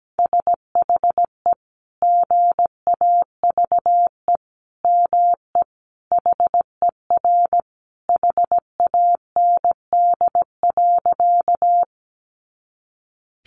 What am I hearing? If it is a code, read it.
Morse code, heard as a keyed tone, about 17 wpm, 700 Hz. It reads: SHE GAVE ME HER HAND.
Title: Sentences from Top 500 Words 17wpm